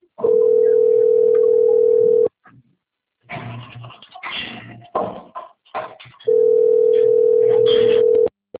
Ringback tone (North America), 440 + 480 Hz.